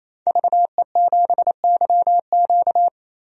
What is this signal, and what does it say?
Morse code, heard as a keyed tone, about 28 wpm, 700 Hz. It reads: VE7YQ
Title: DXCC-100 Call Signs - 5 Characters - 28wpm